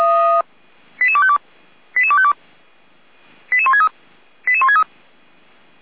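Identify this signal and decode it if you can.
Five-tone selective call (ZVEI), heard as quick runs of five tones, tones of about 70 ms. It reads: 88252 88252 88163 88163
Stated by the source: Five-tone ZVEI selective call received by SDR, from the fire brigade (Feuerwehr) Orlamünde server, channel 455.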